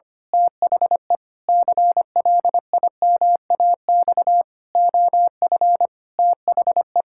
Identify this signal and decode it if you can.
Morse code, heard as a keyed tone, about 25 wpm, 700 Hz. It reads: THE CLIMAX OF THE
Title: Pangram Challenge v2 - 25wpm